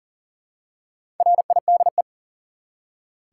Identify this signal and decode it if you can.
Morse code, heard as a keyed tone, about 40 wpm, 700 Hz. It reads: RIDE
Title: Four Letter Words - 40wpm